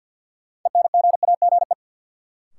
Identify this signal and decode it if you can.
Morse code, heard as a keyed tone, about 51 wpm, 700 Hz. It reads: ENGAGE
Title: Sentences from Top 1500 Words - New Word Drill Alt - 50wpm